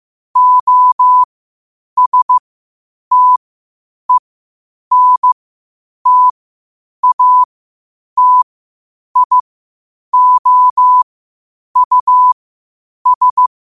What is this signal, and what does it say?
Morse code, heard as a keyed tone, about 15 wpm, spaced as for 8 wpm, 1000 Hz. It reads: OSTENTATIOUS
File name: 8wpm